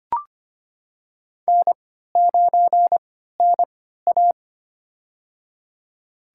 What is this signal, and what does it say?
Morse code, heard as a keyed tone, about 25 wpm, spaced as for 14 wpm, 700 Hz. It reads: N9NA
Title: USA Call Signs - 4 Characters - 3x Character Spacing - 25wpm